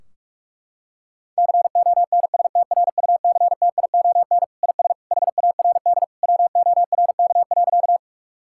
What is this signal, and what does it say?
Morse code, heard as a keyed tone, about 46 wpm, 700 Hz. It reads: CONSTRUCTION IS HARD WORK.
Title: Sentences from Top 1600 Words 45wpm